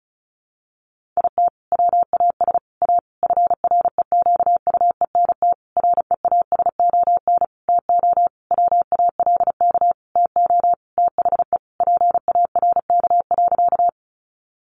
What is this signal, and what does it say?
Morse code, heard as a keyed tone, about 34 wpm, 700 Hz. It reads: IT WAS A FREQUENT REASON TO WALK TO THE PARK.
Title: Sentences from Top 1400 Words 35wpm